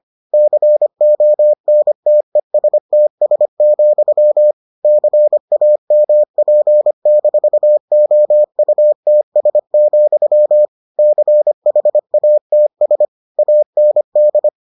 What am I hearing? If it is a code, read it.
Morse code, heard as a keyed tone, about 25 wpm, 600 Hz. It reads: CONTESTS, CAMP-OUTS, CHATS AND